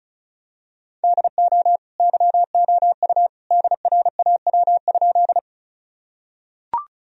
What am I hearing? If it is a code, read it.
Morse code, heard as a keyed tone, about 36 wpm, 700 Hz. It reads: DO YOU DRAW?
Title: Sentences from Top 1000 Words - Review Part 1 - 35wpm